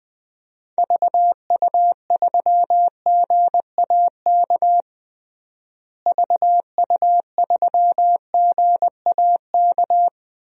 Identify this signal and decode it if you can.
Morse code, heard as a keyed tone, about 20 wpm, 700 Hz. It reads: VU3GAK VU3GAK
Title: DXCC-100 Call Signs - 6 Characters - Repeated 2x - 20wpm